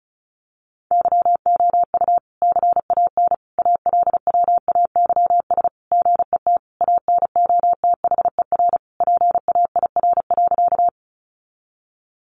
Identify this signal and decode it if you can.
Morse code, heard as a keyed tone, about 34 wpm, 700 Hz. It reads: YOU CAN ALWAYS GET ANOTHER PAIR.
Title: Sentences from Top 1200 Words 35wpm